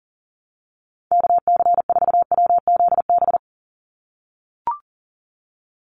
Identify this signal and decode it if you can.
Morse code, heard as a keyed tone, about 41 wpm, 700 Hz. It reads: KC4WZB
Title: NA SOTA Activator Callsigns 40wpm